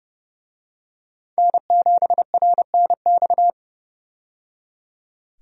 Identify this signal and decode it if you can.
Morse code, heard as a keyed tone, about 30 wpm, 700 Hz. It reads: N7RNX
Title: USA Call Signs - 5 Characters - 30wpm